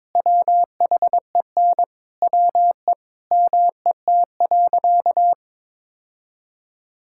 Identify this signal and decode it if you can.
Morse code, heard as a keyed tone, about 22 wpm, 700 Hz. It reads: WHEN WE MET.
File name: practice / 22wpm / Sentences with 2-Word Phrases - 22wpm